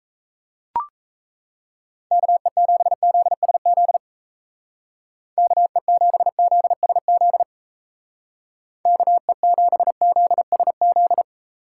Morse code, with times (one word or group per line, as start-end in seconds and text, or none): 2.1-4.0 KE7ZSZ
5.4-7.4 KE7ZSZ
8.9-11.2 KE7ZSZ